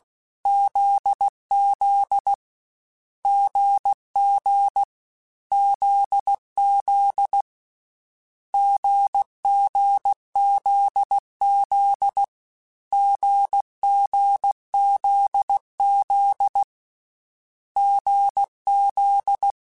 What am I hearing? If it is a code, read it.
Morse code, heard as a keyed tone, about 16 wpm, 800 Hz. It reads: ZZ GG ZZ GGZZ GGZZ GZ